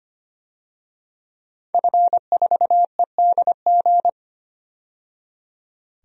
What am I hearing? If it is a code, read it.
Morse code, heard as a keyed tone, about 25 wpm, 700 Hz. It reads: F4EDG
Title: DXCC-100 Call Signs - 5 Characters - 25wpm